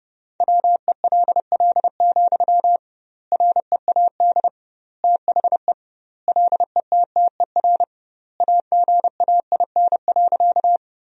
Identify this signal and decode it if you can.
Morse code, heard as a keyed tone, about 30 wpm, 700 Hz. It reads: WELL, READ THE LETTER AGAIN.
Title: 2x Word-Spacing - Sentences from Top 300 Words - 30wpm